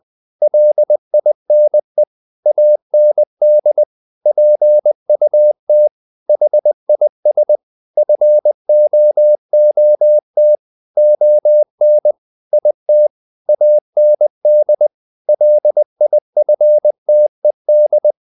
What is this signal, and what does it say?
Morse code, heard as a keyed tone, about 20 wpm, 600 Hz. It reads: LINE AND PUT HIS FOOT ON IT AND LIFTED
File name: the old man and the sea part 2 20wpm